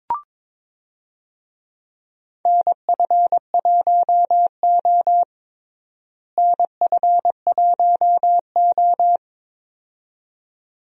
Morse code, heard as a keyed tone, about 22 wpm, 700 Hz. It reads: NF1O NF1O
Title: USA Call Signs - 4 Characters - Repeated 2x - 22wpm